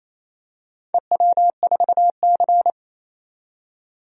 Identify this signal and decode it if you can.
Morse code, heard as a keyed tone, about 28 wpm, 700 Hz. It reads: EW4C
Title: DXCC-100 Call Signs - 4 Characters - 28wpm